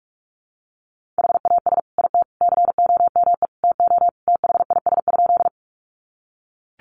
Morse code, heard as a keyed tone, about 44 wpm, 700 Hz. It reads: HAS IT COME TO THIS?
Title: Sentences from Top 100 Words 45wpm